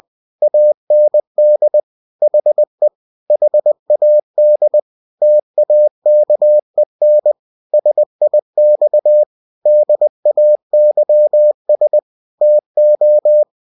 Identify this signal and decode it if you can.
Morse code, heard as a keyed tone, about 20 wpm, 600 Hz. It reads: AND HE HAD TAKEN SIX DAYS TO
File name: winnie-20wpm